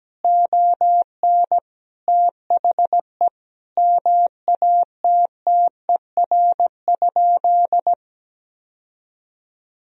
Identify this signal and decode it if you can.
Morse code, heard as a keyed tone, about 17 wpm, 700 Hz. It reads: ON THE MATTER?
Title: Sentences from Top 1000 Words - Review Part 4 - 17wpm